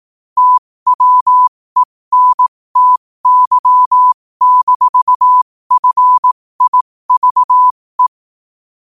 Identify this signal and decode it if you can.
Morse code, heard as a keyed tone, about 18 wpm, spaced as for 15 wpm, 1000 Hz. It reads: TWENTY-FIVE